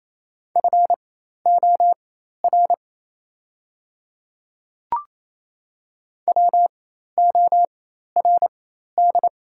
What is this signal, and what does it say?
Morse code, heard as a keyed tone, about 28 wpm, spaced as for 13 wpm, 700 Hz. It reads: FOR WORD
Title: 4x Character Spacing - Top 100 Words - 28wpm